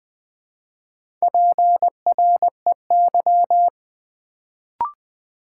Morse code, heard as a keyed tone, about 20 wpm, 700 Hz. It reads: PREY
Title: Build A Word - 1 Letter At A Time - 3 to 5-Letter Words - 20wpm